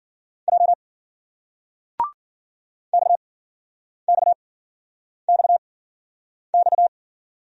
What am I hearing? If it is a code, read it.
Morse code, heard as a keyed tone, about 51 wpm, 700 Hz. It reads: J X X X X